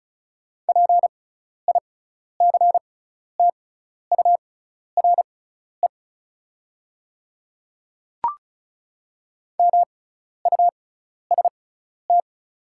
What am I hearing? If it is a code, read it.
Morse code, heard as a keyed tone, about 36 wpm, spaced as for 12 wpm, 690 Hz. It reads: PICTURE MUST
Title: 6x Character Spacing - Top 200 Words - 35wpm